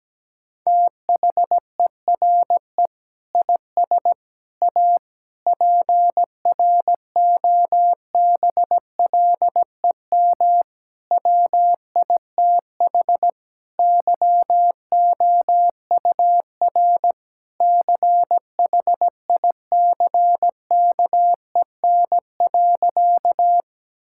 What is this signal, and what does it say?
Morse code, heard as a keyed tone, about 17 wpm, 700 Hz. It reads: THERE IS A PROBLEM WITH YOUR CHICKEN.
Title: Sentences from Top 1000 Words - Review Part 5 - 17wpm